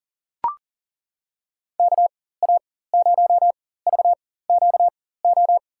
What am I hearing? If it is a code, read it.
Morse code, heard as a keyed tone, about 40 wpm, spaced as for 19 wpm, 700 Hz. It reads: KA0VQO
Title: USA Call Signs - 6 Characters - 4x Character Spacing - 40wpm